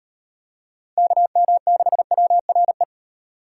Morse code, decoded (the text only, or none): KM6WRE